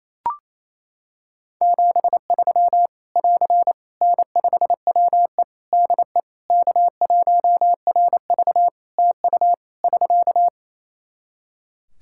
Morse code, heard as a keyed tone, about 28 wpm, 700 Hz. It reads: 73 <AR> N5WE DE K1RV TU <SK>
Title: Practice QSO - Exchange 4 - Ending - 28wpm